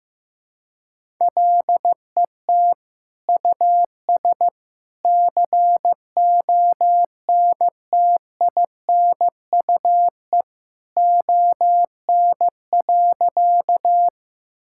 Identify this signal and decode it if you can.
Morse code, heard as a keyed tone, about 15 wpm, 700 Hz. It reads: LET US CONTINUE ON.
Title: Sentences from Top 1000 Words - Review Part 6 - 15wpm